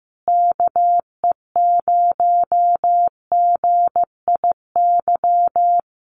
Morse code, heard as a keyed tone, about 16 wpm, 700 Hz. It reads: KE0GIY
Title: Common QSO Phrases Encore 15wpm